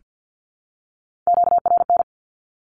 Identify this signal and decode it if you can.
Morse code, heard as a keyed tone, about 48 wpm, 700 Hz. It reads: QRN
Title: Q-Codes 50wpm